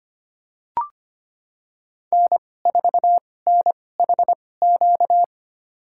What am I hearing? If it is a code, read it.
Morse code, heard as a keyed tone, about 25 wpm, spaced as for 18 wpm, 700 Hz. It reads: N4NHQ
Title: USA Call Signs - 2x Character Spacing - 25wpm